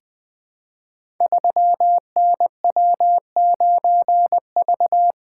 Morse code, heard as a keyed tone, about 20 wpm, 700 Hz. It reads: 3NW9V
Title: Code Groups - 5 Letter:Numbers - 20wpm